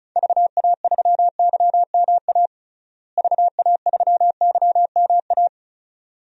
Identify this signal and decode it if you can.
Morse code, heard as a keyed tone, about 36 wpm, 700 Hz. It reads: VA3YMA VA3YMA
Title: DXCC-100 Call Signs - 6 Characters - Repeated 2x - 35wpm